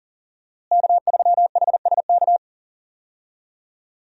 Morse code, heard as a keyed tone, about 39 wpm, 700 Hz. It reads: K3HSK
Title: USA Call Signs - 5 Characters - Repeated 2x - 40wpm